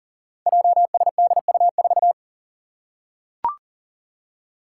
Morse code, heard as a keyed tone, about 42 wpm, 690 Hz. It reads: JSDU4
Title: Five Challenging Letter-Number Combinations 40wpm